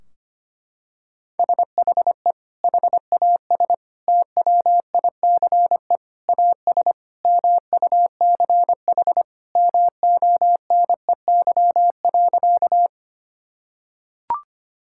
Morse code, heard as a keyed tone, about 25 wpm, 700 Hz. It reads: SHE HAS TWICE AS MUCH MONEY.
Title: Sentences from Top 1600 Words 25wpm